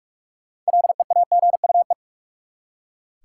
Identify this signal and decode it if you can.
Morse code, heard as a keyed tone, about 42 wpm, 700 Hz. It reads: LEAGUE
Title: Six Letter Words - 45wpm